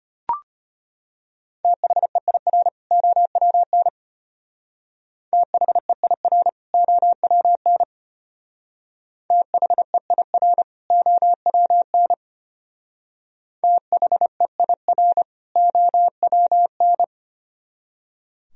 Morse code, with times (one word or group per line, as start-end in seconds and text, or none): 1.6-2.7 THEIR
2.9-3.9 OWN
5.3-6.5 THEIR
6.7-7.8 OWN
9.3-10.6 THEIR
10.9-12.1 OWN
13.6-15.2 THEIR
15.6-17.0 OWN